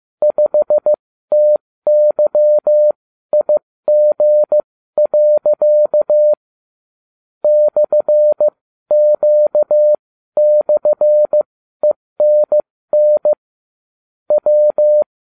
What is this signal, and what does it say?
Morse code, heard as a keyed tone, about 15 wpm, spaced as for 12 wpm, 600 Hz. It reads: 5TYIG. /Q/ENN W